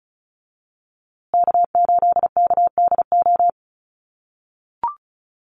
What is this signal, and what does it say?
Morse code, heard as a keyed tone, about 34 wpm, 700 Hz. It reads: K8KDO
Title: USA Call Signs Repeated 2x 35wpm